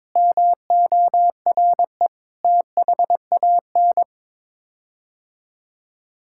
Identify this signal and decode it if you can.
Morse code, heard as a keyed tone, about 22 wpm, 700 Hz. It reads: MORE THAN
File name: practice / 22wpm / Sets of 7 Words - 22wpm